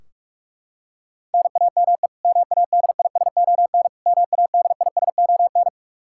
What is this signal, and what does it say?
Morse code, heard as a keyed tone, about 41 wpm, 700 Hz. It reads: NAME MADISON MADISON
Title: K1USN SST QSO Practice 45wpm